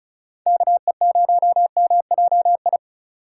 Morse code, heard as a keyed tone, about 35 wpm, 700 Hz. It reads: KE0MJI
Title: USA Call Signs - 6 Characters - 3x Character Spacing - 35wpm